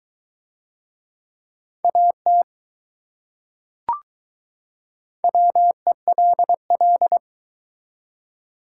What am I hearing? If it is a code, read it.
Morse code, heard as a keyed tone, about 20 wpm, 700 Hz. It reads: AT WELL